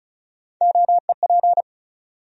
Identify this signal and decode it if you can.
Morse code, heard as a keyed tone, about 36 wpm, 700 Hz. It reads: OEP
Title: USA Call Sign Postfixes - 3 Letters 35wpm